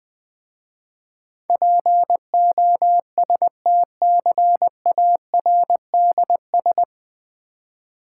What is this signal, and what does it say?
Morse code, heard as a keyed tone, about 20 wpm, 700 Hz. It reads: POSTCARDS
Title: Difficult Words with Hints - 20wpm